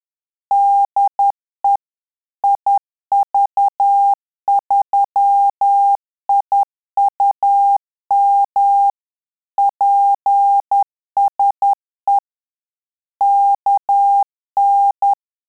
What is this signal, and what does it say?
Morse code, heard as a keyed tone, about 10 wpm, 800 Hz. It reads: DE IV3IUM PSE KN
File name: Telegraphy-group / QSO CW 10 PPM